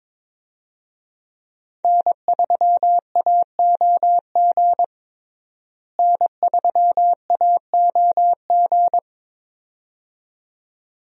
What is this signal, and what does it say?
Morse code, heard as a keyed tone, about 22 wpm, 700 Hz. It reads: N3AOG N3AOG
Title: USA Call Signs - 5 Characters - Repeated 2x - 22wpm